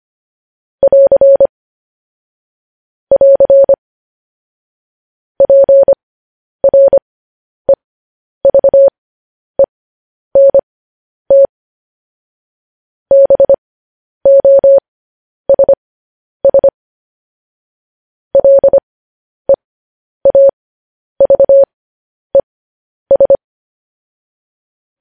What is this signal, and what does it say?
Morse code, heard as a keyed tone, about 25 wpm, spaced as for 10 wpm, 550 Hz. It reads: <AR> <AR> PREVENT BOSS LEAVES
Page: news-headlines-in-morse-code-at-10-wpm